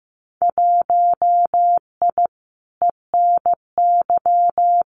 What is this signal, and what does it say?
Morse code, heard as a keyed tone, about 15 wpm, 700 Hz. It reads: 1I ENY
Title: Winter Field Day Exchanges Encore 15wpm